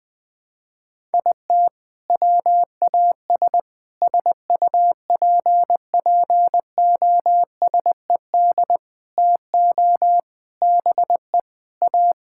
Morse code, heard as a keyed tone, about 20 wpm, 700 Hz. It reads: IT WAS SUPPOSED TO BE A
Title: Sentences with 4-Word Phrases - 20wpm